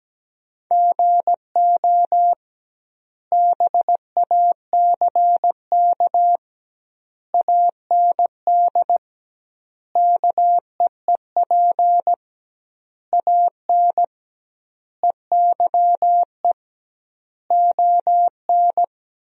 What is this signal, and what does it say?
Morse code, heard as a keyed tone, about 17 wpm, 700 Hz. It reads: GO BACK AND KEEP AN EYE ON
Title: 2x Word-Spacing - Sentences from Top 300 Words - 17wpm